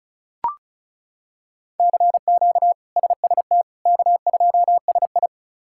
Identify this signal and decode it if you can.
Morse code, heard as a keyed tone, about 34 wpm, 700 Hz. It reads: CQ SST K2SI
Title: K1USN SST QSO Practice - 35wpm